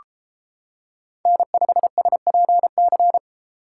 Morse code, heard as a keyed tone, about 29 wpm, 700 Hz. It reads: N5SPC